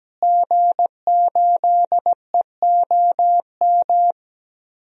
Morse code, heard as a keyed tone, about 17 wpm, 700 Hz. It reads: G8EOM
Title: DXCC-100 Call Signs - 5 Characters - Repeated 2x - 17wpm